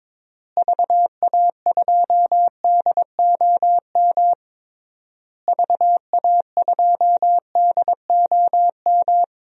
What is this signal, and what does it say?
Morse code, heard as a keyed tone, about 22 wpm, 700 Hz. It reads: VA2DOM VA2DOM
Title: DXCC-100 Call Signs - 6 Characters - Repeated 3x - 22wpm